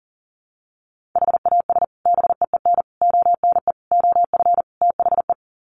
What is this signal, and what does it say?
Morse code, heard as a keyed tone, about 40 wpm, 700 Hz. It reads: HAS BEEN ONE OF THE
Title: Sets of 5 Words Encore 40wpm